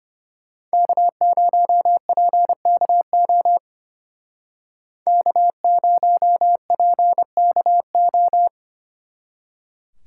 Morse code, heard as a keyed tone, about 30 wpm, 700 Hz. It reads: K0PKO K0PKO
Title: Speed Racing - USA Call Signs 25wpm